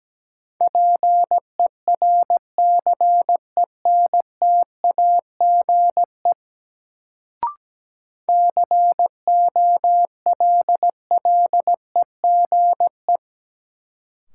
Morse code, heard as a keyed tone, about 17 wpm, 700 Hz. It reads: PERCENTAGE COLLEGE
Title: Sentences from Top 1300 Words - New Word Drill Alt - 17wpm